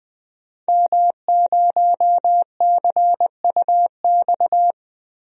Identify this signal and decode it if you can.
Morse code, heard as a keyed tone, about 20 wpm, 700 Hz. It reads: M0CUX
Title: DXCC-100 Call Signs - 5 Characters - Repeated 3x - 20wpm